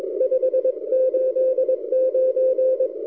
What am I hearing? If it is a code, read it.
Morse code, heard as a keyed tone, about 21 wpm, 520 Hz. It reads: 589
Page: during this recording I switched the rx loop direction front to the backside K3ZM almost disappeared so my K9AY loop seems to work okay recorded amazing 160m signal